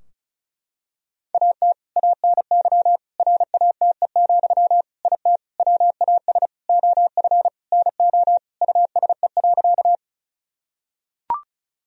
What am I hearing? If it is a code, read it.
Morse code, heard as a keyed tone, about 36 wpm, 700 Hz. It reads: AT ANY RATE, IT WAS OF NO USE.
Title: Sentences from Top 1000 Words - Review Part 4 - 35wpm